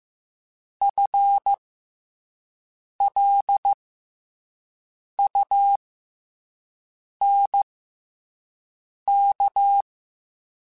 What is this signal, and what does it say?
Morse code, heard as a keyed tone, about 15 wpm, spaced as for 5 wpm, 800 Hz. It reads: FLUNK
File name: FL Runs / FL Runs mp3s / FL Runs @5